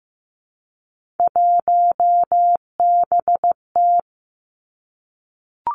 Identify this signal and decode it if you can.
Morse code, heard as a keyed tone, about 15 wpm, 700 Hz. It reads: 1BT
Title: Three Challenging Letter-Number Combinations 15wpm